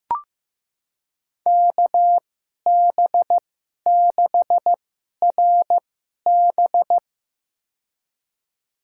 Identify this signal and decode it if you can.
Morse code, heard as a keyed tone, about 15 wpm, spaced as for 11 wpm, 700 Hz. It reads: KB6RB
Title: USA Call Signs - 5 Characters - 2x Character Spacing - 15wpm